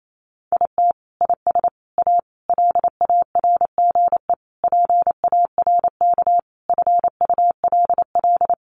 Morse code, heard as a keyed tone, about 28 wpm, 700 Hz. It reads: IT IS A LARGE PARK FULL
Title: Sentences from Top 1300 Words 28wpm